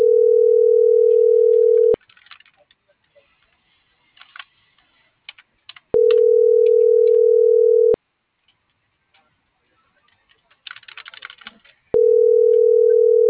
Ringback tone (North America), 440 + 480 Hz.